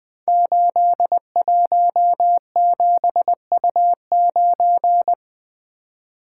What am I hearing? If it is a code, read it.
Morse code, heard as a keyed tone, about 20 wpm, 700 Hz. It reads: 817U9
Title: Code Groups - 5 Letter:Numbers - 20wpm